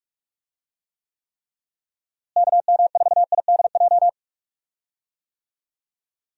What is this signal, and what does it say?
Morse code, heard as a keyed tone, about 46 wpm, 700 Hz. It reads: KM4IDJ
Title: USA Call Signs - 6 Characters - Repeated 3x - 45wpm